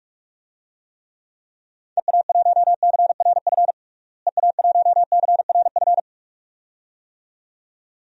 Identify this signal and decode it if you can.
Morse code, heard as a keyed tone, about 44 wpm, 700 Hz. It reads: EA1CRF EA1CRF
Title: DXCC-100 Call Signs - 6 Characters - Repeated 2x - 45wpm